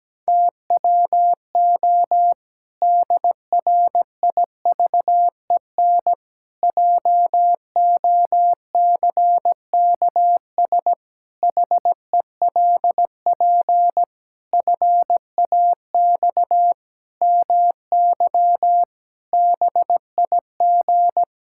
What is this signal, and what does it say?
Morse code, heard as a keyed tone, about 17 wpm, 700 Hz. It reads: TWO DRIVEN JOCKS HELP FAX MY BIG
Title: Pangram Challenge v2 - 17wpm